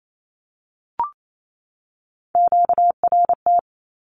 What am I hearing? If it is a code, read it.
Morse code, heard as a keyed tone, about 28 wpm, 700 Hz. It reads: QRT
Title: Q-Codes 28wpm